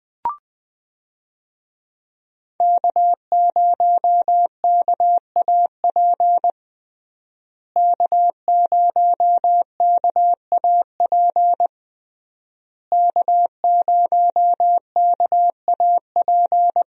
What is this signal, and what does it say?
Morse code, heard as a keyed tone, about 20 wpm, 700 Hz. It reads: K0KAP K0KAP K0KAP
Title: USA Call Signs - 5 Characters - Repeated 3x - 20wpm